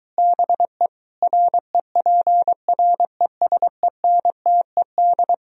Morse code, heard as a keyed tone, about 23 wpm, 700 Hz. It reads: BE REPRESENTED